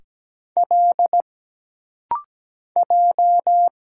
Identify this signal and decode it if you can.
Morse code, heard as a keyed tone, about 17 wpm, 700 Hz. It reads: L J
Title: Single Letter-Number Alt - Rapid-Fire 17wpm